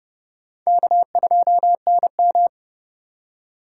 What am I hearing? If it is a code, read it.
Morse code, heard as a keyed tone, about 29 wpm, 700 Hz. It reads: K2NM